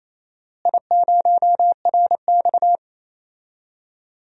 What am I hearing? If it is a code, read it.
Morse code, heard as a keyed tone, about 28 wpm, 700 Hz. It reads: I0RX
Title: DXCC-100 Call Signs - 4 Characters - Repeated 2x - 28wpm